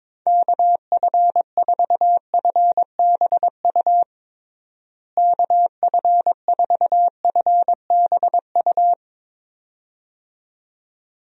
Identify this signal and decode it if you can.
Morse code, heard as a keyed tone, about 22 wpm, 700 Hz. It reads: KF4FBU KF4FBU
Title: USA Call Signs - 6 Characters - Repeated 2x - 22wpm